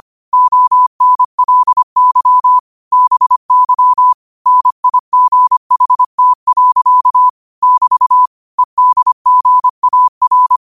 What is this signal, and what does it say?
Morse code, heard as a keyed tone, about 25 wpm, 1000 Hz. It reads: ONLY BY NIGHT. <BT> EDGAR
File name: qotdfiles / 25wpm